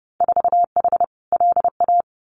Morse code, heard as a keyed tone, about 30 wpm, 700 Hz. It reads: 4H LA
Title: Winter Field Day Exchanges Encore 30wpm